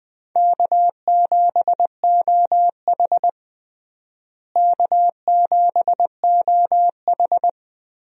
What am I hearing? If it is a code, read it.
Morse code, heard as a keyed tone, about 20 wpm, 690 Hz. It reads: K7OH K7OH
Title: USA Call Signs - 4 Characters - Repeated 3x - 20wpm